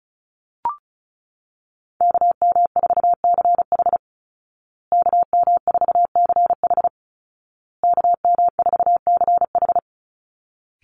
Morse code, heard as a keyed tone, about 34 wpm, 700 Hz. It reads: KM4CH KM4CH KM4CH
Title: Call Signs Repeated 2x - 35wpm